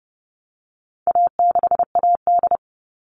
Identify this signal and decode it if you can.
Morse code, heard as a keyed tone, about 30 wpm, 700 Hz. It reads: A6AD
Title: All Cut Numbers - 4-Digit 30wpm